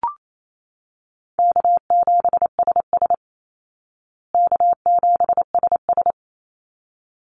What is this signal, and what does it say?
Morse code, heard as a keyed tone, about 28 wpm, 700 Hz. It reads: K7SS K7SS
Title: Easy Call Signs Repeated 2x - 28wpm